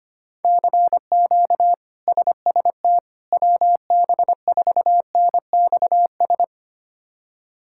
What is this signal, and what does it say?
Morse code, heard as a keyed tone, about 25 wpm, 700 Hz. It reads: CQ SST WB4NXS
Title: K1USN SST QSO Practice - 25wpm